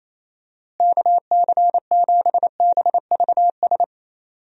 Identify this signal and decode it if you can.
Morse code, heard as a keyed tone, about 28 wpm, 700 Hz. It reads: KC7BVS
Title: USA Call Signs - 6 Characters - Repeated 2x - 28wpm